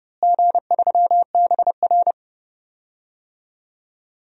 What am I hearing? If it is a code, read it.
Morse code, heard as a keyed tone, about 29 wpm, 700 Hz. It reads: G3BR
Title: DXCC-100 Call Signs - 4 Characters - Repeated 3x - 30wpm